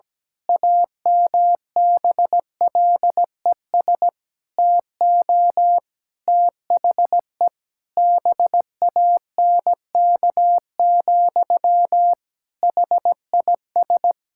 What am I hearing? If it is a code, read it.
Morse code, heard as a keyed tone, about 17 wpm, 700 Hz. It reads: AMBLES TO THE BANK, HIS
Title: Pangram Challenge v2 - 17wpm